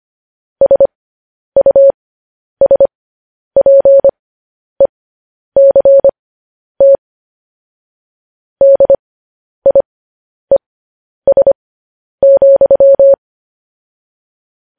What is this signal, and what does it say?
Morse code, heard as a keyed tone, about 25 wpm, spaced as for 10 wpm, 550 Hz. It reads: SUSPECT DIES,
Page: news headlines in morse code at 10 wpm